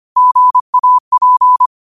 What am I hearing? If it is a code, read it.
Morse code, heard as a keyed tone, about 25 wpm, 1000 Hz. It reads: GAP